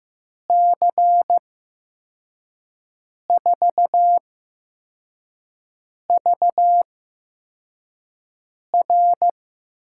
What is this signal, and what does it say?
Morse code, heard as a keyed tone, about 15 wpm, 700 Hz. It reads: C 4 V R
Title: USA Call Signs - 5 Characters - 8x Character Spacing - 15wpm